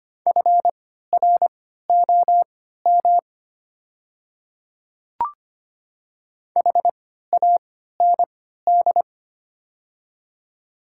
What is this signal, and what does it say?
Morse code, heard as a keyed tone, about 25 wpm, spaced as for 14 wpm, 690 Hz. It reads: FROM HAND
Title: 3x Character Spacing - Top 200 Words - 25wpm